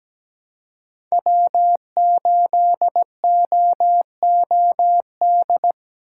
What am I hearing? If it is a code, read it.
Morse code, heard as a keyed tone, about 17 wpm, 700 Hz. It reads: W8OOD
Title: USA Call Signs - 5 Characters - 2x Character Spacing - 17wpm